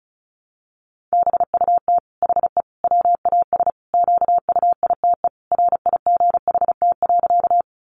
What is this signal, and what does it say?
Morse code, heard as a keyed tone, about 34 wpm, 700 Hz. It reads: BUT HE WAS QUITE RIGHT.
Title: Sentences from Top 700 Words 35wpm